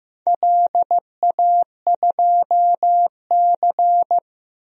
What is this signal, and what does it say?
Morse code, heard as a keyed tone, about 15 wpm, 700 Hz. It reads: LA2C
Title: DXCC-100 Call Signs - 4 Characters - 15wpm